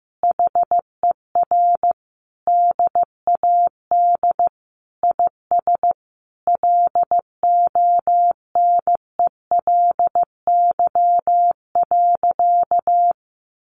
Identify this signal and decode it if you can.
Morse code, heard as a keyed tone, about 15 wpm, 700 Hz. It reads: HER DAD IS LONELY.
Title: Additional Easy Phrases 15wpm